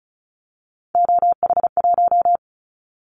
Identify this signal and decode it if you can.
Morse code, heard as a keyed tone, about 34 wpm, 700 Hz. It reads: OH1
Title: Callsign Prefixes for Top 50 DXCC - 35wpm